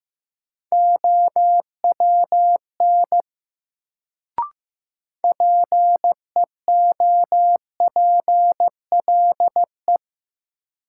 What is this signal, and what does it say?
Morse code, heard as a keyed tone, about 15 wpm, 700 Hz. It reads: OWN PEOPLE
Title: Sets of 2 Words - New Word Drill - 15wpm